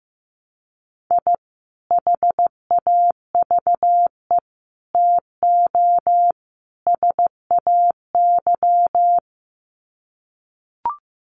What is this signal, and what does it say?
Morse code, heard as a keyed tone, about 15 wpm, 700 Hz. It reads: I HAVE TO SAY
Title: Sets of 4 Words 15wpm